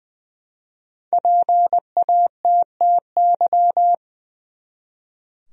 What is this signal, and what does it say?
Morse code, heard as a keyed tone, about 20 wpm, 700 Hz. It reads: PATTY
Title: USA Common Names - 20wpm